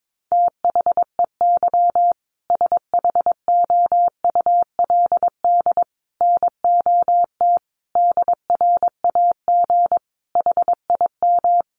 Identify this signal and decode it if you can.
Morse code, heard as a keyed tone, about 22 wpm, 700 Hz. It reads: THEY SHOULD NOT DRAG HIM
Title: Sentences from Top 1500 Words 22wpm